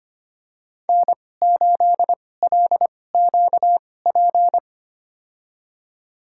Morse code, heard as a keyed tone, about 25 wpm, spaced as for 18 wpm, 700 Hz. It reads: N8LQP
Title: USA Call Signs - 5 Characters - 2x Character Spacing - 25wpm